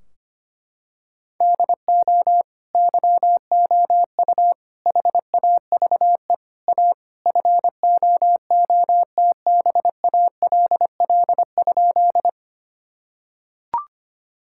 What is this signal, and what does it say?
Morse code, heard as a keyed tone, about 25 wpm, 700 Hz. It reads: DO YOU HAVE A FOOTBALL?
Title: Sentences from Top 1600 Words 25wpm